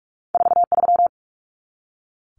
Morse code, heard as a keyed tone, about 46 wpm, 700 Hz. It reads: 43
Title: Common Cut Numbers - 2-Digit 45wpm